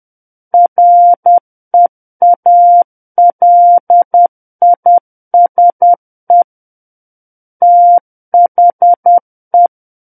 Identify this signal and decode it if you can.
Morse code, heard as a keyed tone, about 10 wpm, 700 Hz. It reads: REALISE THE
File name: Book One / 01-Chapter 1 - 10 WPM